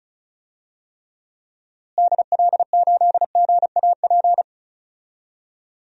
Morse code, heard as a keyed tone, about 36 wpm, 700 Hz. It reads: DL8GAP
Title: DXCC-100 Call Signs - 6 Characters - Repeated 2x - 35wpm